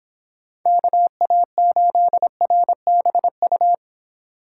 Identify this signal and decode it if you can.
Morse code, heard as a keyed tone, about 30 wpm, 700 Hz. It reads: KA8RBU